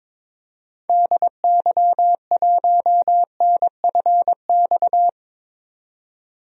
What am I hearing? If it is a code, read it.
Morse code, heard as a keyed tone, about 22 wpm, 700 Hz. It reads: DY1NFX
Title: DXCC-100 Call Signs - 6 Characters - 22wpm